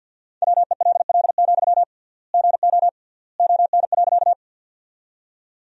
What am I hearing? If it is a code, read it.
Morse code, heard as a keyed tone, about 51 wpm, 700 Hz. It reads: WELL, GO ON.
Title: 3x Word-Spacing - Sentences from Top 200 Words - 50wpm